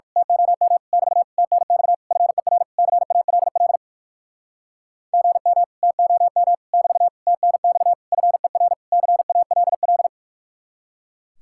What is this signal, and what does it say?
Morse code, heard as a keyed tone, about 53 wpm, 700 Hz. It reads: TOM <BT> TNX FER CALL GM TOM <BT> TNX FER CALL
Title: Speed-Racing Common QSO Phrases - 45wpm